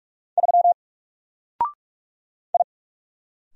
Morse code, heard as a keyed tone, about 42 wpm, 700 Hz. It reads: 3 I
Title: Single Letter-Number 45wpm